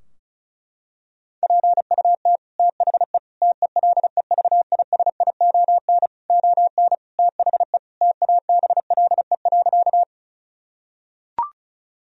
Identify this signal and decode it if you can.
Morse code, heard as a keyed tone, about 36 wpm, 700 Hz. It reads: PUT THE TELEVISION ON THE TABLE.
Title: Sentences from Top 1000 Words - Review Part 4 - 35wpm